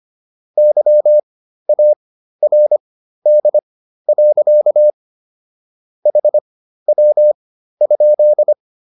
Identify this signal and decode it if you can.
Morse code, heard as a keyed tone, about 25 wpm, spaced as for 13 wpm, 600 Hz. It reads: YARD. HW?